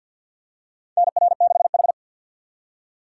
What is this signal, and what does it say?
Morse code, heard as a keyed tone, about 46 wpm, 700 Hz. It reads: NR6H
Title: USA Call Signs - 4 Characters - 2x Character Spacing - 50wpm